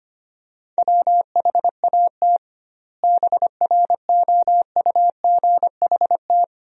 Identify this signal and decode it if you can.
Morse code, heard as a keyed tone, about 25 wpm, 700 Hz. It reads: WHAT BROUGHT
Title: 2x Word-Spacing - Sentences from Top 500 Words - 25wpm